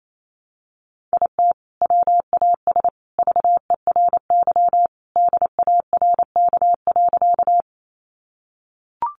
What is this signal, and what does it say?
Morse code, heard as a keyed tone, about 28 wpm, 700 Hz. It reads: IT WAS VERY DARK.
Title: Sentences from Top 500 Words 28wpm